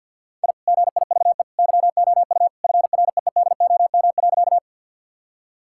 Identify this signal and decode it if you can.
Morse code, heard as a keyed tone, about 50 wpm, 700 Hz. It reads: I GIVE YOU FREEDOM.
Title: Sentences from Top 1000 Words - Review Part 5 - 50wpm